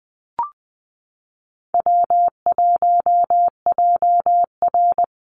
Morse code, heard as a keyed tone, about 20 wpm, 700 Hz. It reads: W1JR
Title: Call Signs Repeated 2x - 20wpm